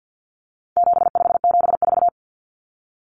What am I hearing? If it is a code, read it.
Morse code, heard as a keyed tone, about 50 wpm, 700 Hz. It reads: 7574
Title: Common Cut Numbers - 4-Digit 50wpm